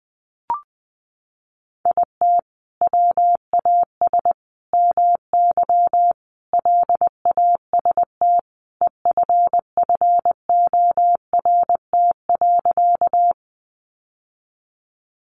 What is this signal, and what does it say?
Morse code, heard as a keyed tone, about 20 wpm, 700 Hz. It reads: IT WAS MY LAST EFFORT.
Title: Sentences from Top 1200 Words 20wpm